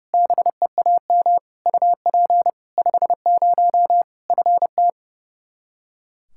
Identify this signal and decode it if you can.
Morse code, heard as a keyed tone, about 32 wpm, 700 Hz. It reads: BEAM UP 50 FT
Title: K1USN SST QSO Practice 30wpm